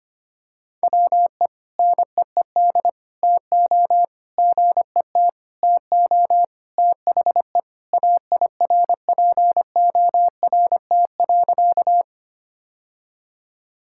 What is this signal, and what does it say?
Morse code, heard as a keyed tone, about 25 wpm, 700 Hz. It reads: WE NEED TO GET TO THE AIRPORT.
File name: practice / 25wpm / Sentences with 3-Word Phrases - 25wpm